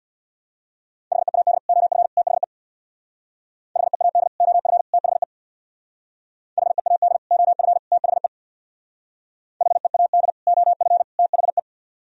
Morse code, heard as a keyed tone, about 62 wpm, 700 Hz. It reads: HEAD OF THE HEAD OF THE HEAD OF THE HEAD OF THE
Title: Speed-Racing - Sets of 3 Words 50wpm